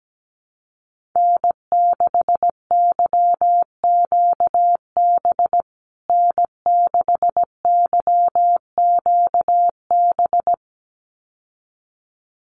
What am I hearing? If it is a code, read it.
Morse code, heard as a keyed tone, about 17 wpm, 700 Hz. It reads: N6YQB N6YQB
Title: USA Call Signs Repeated 2x 17wpm